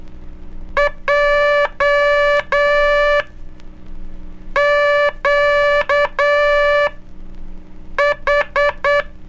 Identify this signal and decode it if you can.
Morse code, heard as a keyed tone, about 9 wpm, spaced as for 5 wpm, 580 Hz. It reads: JQH
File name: cursos / Curso CW mp3 / CW Aula - 04